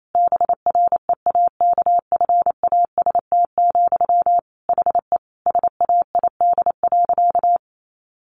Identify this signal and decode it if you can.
Morse code, heard as a keyed tone, about 28 wpm, 700 Hz. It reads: BREAKFAST, HE SAID.
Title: Sentences from Top 1500 Words 28wpm